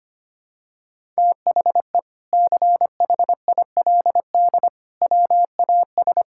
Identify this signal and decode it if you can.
Morse code, heard as a keyed tone, about 25 wpm, 700 Hz. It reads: THE CHILD WAS
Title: Sentences from Top 1000 Words - Review Part 6 - 25wpm